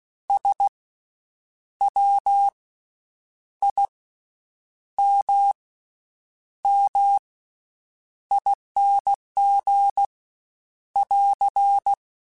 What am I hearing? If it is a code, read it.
Morse code, heard as a keyed tone, about 16 wpm, spaced as for 8 wpm, 800 Hz. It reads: SWIMMING <AR>
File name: Special INGs / Special INGs-mp3s / VerySlowToFaster Ing @15